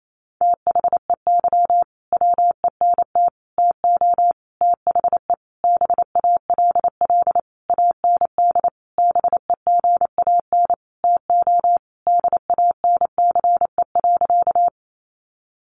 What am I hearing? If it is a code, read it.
Morse code, heard as a keyed tone, about 28 wpm, 700 Hz. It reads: THEY WENT TO THE BALL AND BEGAN TO DANCE.
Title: Sentences from Top 900 Words 28wpm